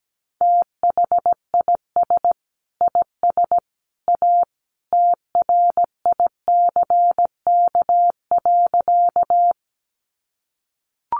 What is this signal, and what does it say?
Morse code, heard as a keyed tone, about 17 wpm, 700 Hz. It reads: THIS IS A TRICK.
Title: Sentences from Top 1500 Words 17wpm